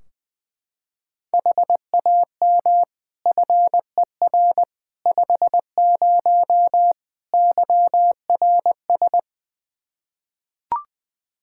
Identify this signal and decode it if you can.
Morse code, heard as a keyed tone, about 20 wpm, 700 Hz. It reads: HAM FER 50 YRS
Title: Practice QSO - Exchange 3 - More Info - 20wpm